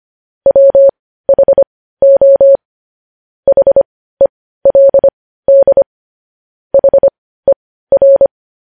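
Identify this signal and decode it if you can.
Morse code, heard as a keyed tone, about 25 wpm, spaced as for 15 wpm, 550 Hz. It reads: WHO HELD HER
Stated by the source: News headlines sent in Morse code at 15 wpm.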